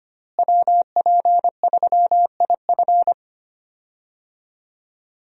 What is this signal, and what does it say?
Morse code, heard as a keyed tone, about 25 wpm, 700 Hz. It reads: WP3IF
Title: USA Call Signs - 5 Characters - Repeated 2x - 25wpm